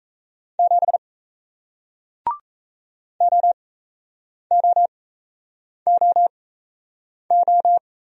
Morse code, heard as a keyed tone, about 37 wpm, spaced as for 8 wpm, 700 Hz. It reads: 7 OOOO